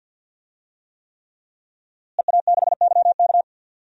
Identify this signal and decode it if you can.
Morse code, heard as a keyed tone, about 52 wpm, 700 Hz. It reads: EA6YK
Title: DXCC-100 Call Signs - 5 Characters - Repeated 3x - 50wpm